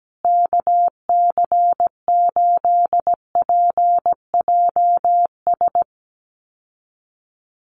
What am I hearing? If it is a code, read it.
Morse code, heard as a keyed tone, about 17 wpm, 700 Hz. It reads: KC8PJS
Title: USA Call Signs Repeated 3x 17wpm